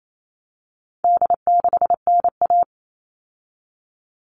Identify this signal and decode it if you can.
Morse code, heard as a keyed tone, about 28 wpm, 700 Hz. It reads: D6NA
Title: All Cut Numbers - 4-Digit 28wpm